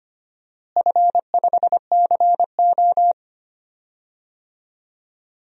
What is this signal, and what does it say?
Morse code, heard as a keyed tone, about 25 wpm, 700 Hz. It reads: F5CO
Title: DXCC-100 Call Signs - 4 Characters - Repeated 2x - 25wpm